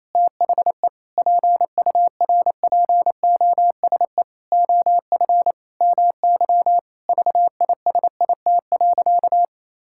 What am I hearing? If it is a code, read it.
Morse code, heard as a keyed tone, about 28 wpm, 700 Hz. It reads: THE PURPOSE OF MY VISIT.
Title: Sentences from Top 1000 Words - Review Part 6 - 28wpm